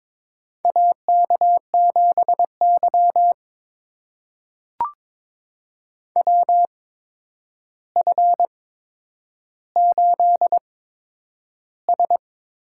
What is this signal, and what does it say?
Morse code, heard as a keyed tone, about 21 wpm, spaced as for 9 wpm, 700 Hz. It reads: AK7Y WF8S
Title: USA Call Signs - 4 Characters - 8x Character Spacing - 22wpm